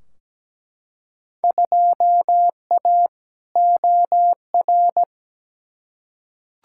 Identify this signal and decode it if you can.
Morse code, heard as a keyed tone, about 17 wpm, 700 Hz. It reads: 2A OR
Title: Field Day Exchanges - 17wpm